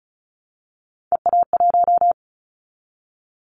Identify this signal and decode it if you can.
Morse code, heard as a keyed tone, about 34 wpm, 700 Hz. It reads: EA1
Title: Callsign Prefixes for Top 25 DXCC - 35wpm